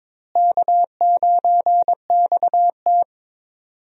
Morse code, heard as a keyed tone, about 22 wpm, 700 Hz. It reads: K9XT
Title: USA Call Signs - 4 Characters - Repeated 2x - 22wpm